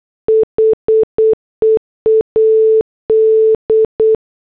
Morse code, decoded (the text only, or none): HEAD